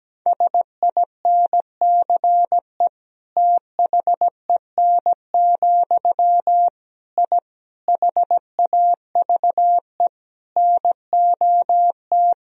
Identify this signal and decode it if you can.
Morse code, heard as a keyed tone, about 17 wpm, 700 Hz. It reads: SINCE THEN, I HAVE NOT
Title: Sentences from Top 1000 Words - Review Part 1 - 17wpm